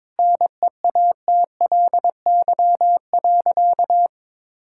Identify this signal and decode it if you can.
Morse code, heard as a keyed tone, about 22 wpm, 690 Hz. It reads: NEATLY.
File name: practice / 22wpm / Easy Sentences with Hints - 22wpm